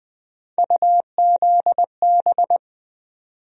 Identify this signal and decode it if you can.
Morse code, heard as a keyed tone, about 20 wpm, 700 Hz. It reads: UZB
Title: USA Call Sign Postfixes - 3 Letters 20wpm